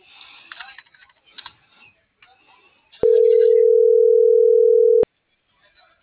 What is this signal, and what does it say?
Ringback tone (North America), 440 + 480 Hz.